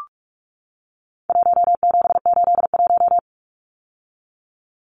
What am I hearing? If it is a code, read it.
Morse code, heard as a keyed tone, about 44 wpm, 700 Hz. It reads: 1781
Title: Numbers - Contest Style - 4-Digit 45wpm